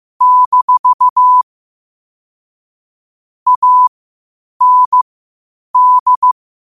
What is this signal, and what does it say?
Morse code, heard as a keyed tone, about 15 wpm, spaced as for 8 wpm, 1000 Hz. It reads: - AND
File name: qotdfiles / 8wpm